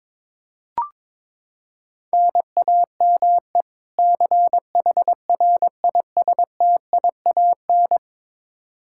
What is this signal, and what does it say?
Morse code, heard as a keyed tone, about 21 wpm, 700 Hz. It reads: NAME CHRISTIAN
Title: K1USN SST QSO Practice 22wpm